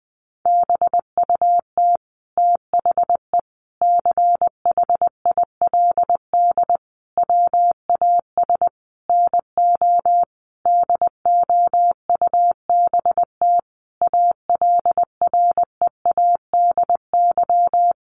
Morse code, heard as a keyed tone, about 20 wpm, 700 Hz. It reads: BUT THE CHILD WAS NO DOUBT ALREADY
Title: Sentences from Top 1500 Words 20wpm